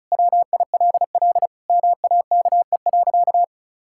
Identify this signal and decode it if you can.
Morse code, heard as a keyed tone, about 34 wpm, 700 Hz. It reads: WILL MAKE.
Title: Sentences from Top 1000 Words - Review Part 5 - 35wpm